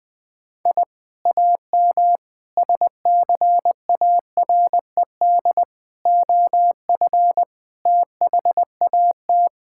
Morse code, heard as a keyed tone, about 20 wpm, 700 Hz. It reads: I AM SCARED OF THAT
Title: Sentences from Top 1000 Words - Review Part 5 - 20wpm